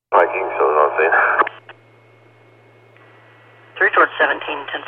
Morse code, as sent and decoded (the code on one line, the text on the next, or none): none